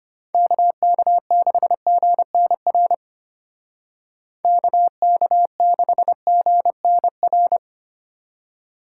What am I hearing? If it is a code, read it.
Morse code, heard as a keyed tone, about 31 wpm, 700 Hz. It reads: KK6GNR KK6GNR
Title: Speed Racing - USA 2x3 Call Signs 25wpm